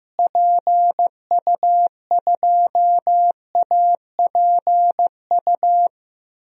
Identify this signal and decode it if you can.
Morse code, heard as a keyed tone, about 15 wpm, 700 Hz. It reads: PU2APU
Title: DXCC-100 Call Signs - 6 Characters - Repeated 2x - 15wpm